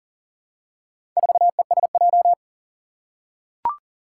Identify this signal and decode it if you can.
Morse code, heard as a keyed tone, about 40 wpm, 700 Hz. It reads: 4ESJ
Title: Four Challenging Letter-Number Combinations 40wpm